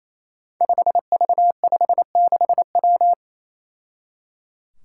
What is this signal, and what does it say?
Morse code, heard as a keyed tone, about 28 wpm, 700 Hz. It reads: 5V56W
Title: Five Challenging Letter-Number Combinations 28wpm